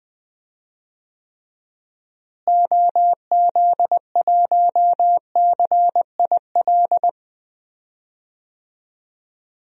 Morse code, heard as a keyed tone, about 20 wpm, 700 Hz. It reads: OZ1CIL
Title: DXCC-100 Call Signs - 6 Characters - Repeated 2x - 20wpm